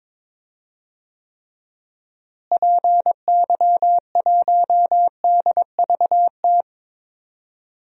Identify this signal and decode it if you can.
Morse code, heard as a keyed tone, about 22 wpm, 700 Hz. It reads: PY1DVT
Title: DXCC-100 Call Signs - 6 Characters - Repeated 3x - 22wpm